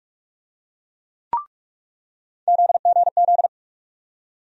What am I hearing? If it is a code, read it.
Morse code, heard as a keyed tone, about 47 wpm, 700 Hz. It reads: ZGZ
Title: USA Call Sign Postfixes - 3 Letters 45wpm